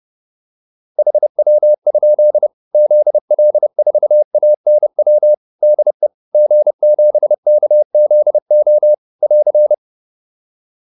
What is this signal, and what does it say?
Morse code, heard as a keyed tone, about 30 wpm, 600 Hz. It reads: HW? ZL4ANW DE G7KZO <AR>
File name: QSO 404 30